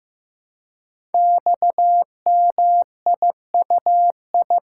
Morse code, heard as a keyed tone, about 15 wpm, 700 Hz. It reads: XMIUI
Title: Code Groups - 5 Letters - 15wpm